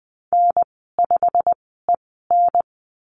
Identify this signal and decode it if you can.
Morse code, heard as a keyed tone, about 20 wpm, spaced as for 14 wpm, 700 Hz. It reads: N5EN
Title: 122 Lesson 20g - Number 3 - Call Signs 20wpm